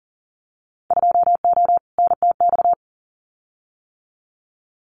Morse code, heard as a keyed tone, about 40 wpm, 700 Hz. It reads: 2O NTX
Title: Winter Field Day Exchanges 40wpm